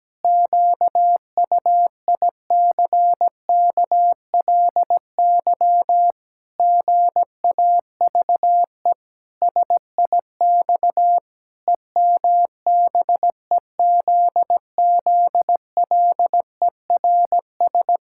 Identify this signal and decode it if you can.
Morse code, heard as a keyed tone, about 17 wpm, 700 Hz. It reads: QUICKLY GAVE SIX EMBEZZLERS
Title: Pangram Challenge v2 - 17wpm